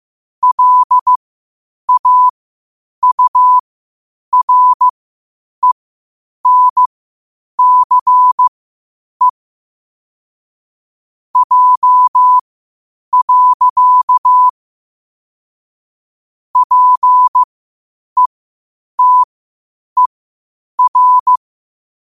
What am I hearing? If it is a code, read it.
Morse code, heard as a keyed tone, about 15 wpm, spaced as for 8 wpm, 1000 Hz. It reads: LAURENCE J. PETER
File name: qotdfiles / 8wpm